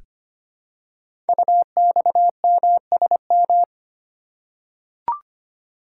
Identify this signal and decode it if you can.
Morse code, heard as a keyed tone, about 25 wpm, 700 Hz. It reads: UXMSM
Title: Code Groups - 5 Letters - 25wpm